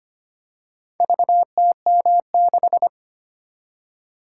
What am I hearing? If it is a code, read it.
Morse code, heard as a keyed tone, about 25 wpm, 700 Hz. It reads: VTM6
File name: practice / 25wpm / Four Letter-Number Combinations - 25wpm